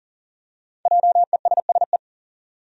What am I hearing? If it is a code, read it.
Morse code, heard as a keyed tone, about 40 wpm, 700 Hz. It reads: JESSE
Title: USA Common Names - 40wpm